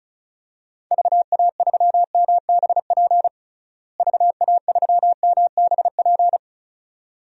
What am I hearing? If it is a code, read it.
Morse code, heard as a keyed tone, about 34 wpm, 700 Hz. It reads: VA3MBP VA3MBP
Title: DXCC-100 Call Signs - 6 Characters - Repeated 2x - 35wpm